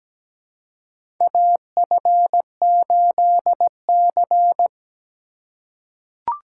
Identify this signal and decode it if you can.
Morse code, heard as a keyed tone, about 17 wpm, 700 Hz. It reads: AF8C
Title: USA Call Signs - 4 Characters - 2x Character Spacing - 17wpm